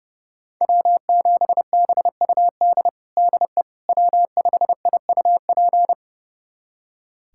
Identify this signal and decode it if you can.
Morse code, heard as a keyed tone, about 32 wpm, 700 Hz. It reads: W7BUD DE W5IUP
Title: USA Callsign DE Callsign - 5 Characters - 30wpm